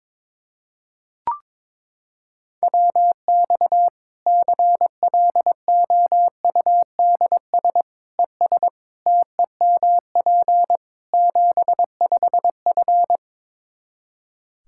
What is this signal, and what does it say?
Morse code, heard as a keyed tone, about 22 wpm, 700 Hz. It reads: WX CLOUDS ES TEMP 75F
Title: Practice QSO - Exchange 2 - Info - 22wpm